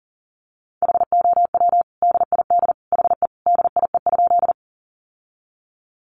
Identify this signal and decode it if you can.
Morse code, heard as a keyed tone, about 40 wpm, 700 Hz. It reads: HOW DID HE DIE?
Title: Sentences from Top 1300 Words 40wpm